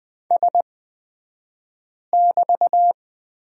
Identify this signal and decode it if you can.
Morse code, heard as a keyed tone, about 20 wpm, spaced as for 5 wpm, 700 Hz. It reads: S<BT>
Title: 235 Lesson 38d - Letter q - Two Characters 20wpm